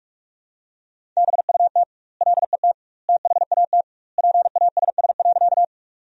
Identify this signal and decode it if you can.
Morse code, heard as a keyed tone, about 43 wpm, 700 Hz. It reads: BUT LET THAT PASS.
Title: 2x Word-Spacing - Sentences from Top 400 Words - 45wpm